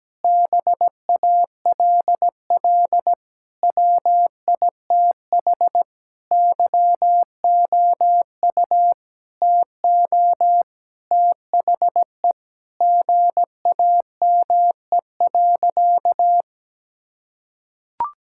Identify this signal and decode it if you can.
Morse code, heard as a keyed tone, about 17 wpm, 700 Hz. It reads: BALL WITH YOU TO THE GAME.
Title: Sentences from Top 1000 Words - Review Part 6 - 17wpm